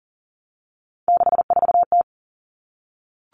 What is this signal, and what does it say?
Morse code, heard as a keyed tone, about 40 wpm, 700 Hz. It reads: 64T
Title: Common Cut Numbers - 3-Digit 40wpm